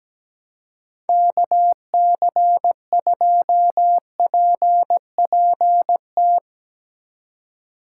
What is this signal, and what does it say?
Morse code, heard as a keyed tone, about 17 wpm, 700 Hz. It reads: KC2PPT
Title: USA Call Signs - 6 Characters - 6x Character Spacing - 17wpm